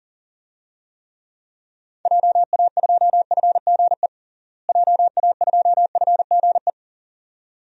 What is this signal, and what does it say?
Morse code, heard as a keyed tone, about 40 wpm, 700 Hz. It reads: JA2FGE JA2FGE
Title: DXCC-100 Call Signs - 6 Characters - Repeated 2x - 40wpm